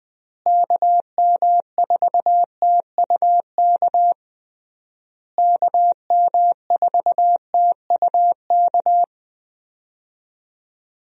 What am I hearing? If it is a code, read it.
Morse code, heard as a keyed tone, about 20 wpm, 700 Hz. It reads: KM4TUK KM4TUK
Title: USA Call Signs - 6 Characters - Repeated 3x - 20wpm